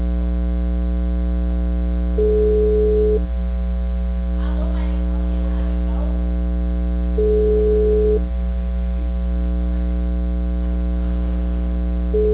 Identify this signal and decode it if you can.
Ringback tone (Europe, 425 Hz).